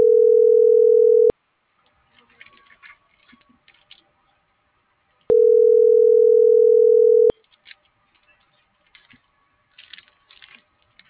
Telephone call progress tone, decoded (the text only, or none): ringback tone